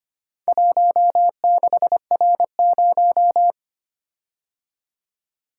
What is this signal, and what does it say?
Morse code, heard as a keyed tone, about 25 wpm, 700 Hz. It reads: 16R0
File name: practice / 25wpm / Four Letter-Number Combinations - 25wpm